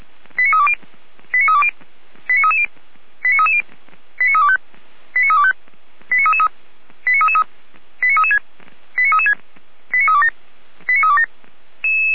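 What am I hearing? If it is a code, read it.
Five-tone selective call (ZVEI), heard as quick runs of five tones, tones of about 70 ms. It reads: 89320 89320 89330 89330 89325 89325 89333 89333 89336 89336 89327 89327
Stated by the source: SDR reception of ZVEI five-tone call 89315, from the feuerwehr-orlamuende.org fire brigade server.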